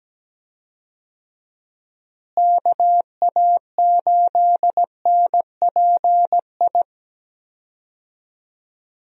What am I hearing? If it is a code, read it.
Morse code, heard as a keyed tone, about 17 wpm, 700 Hz. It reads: KA8NPI
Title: USA Call Signs - 6 Characters - Repeated 2x - 17wpm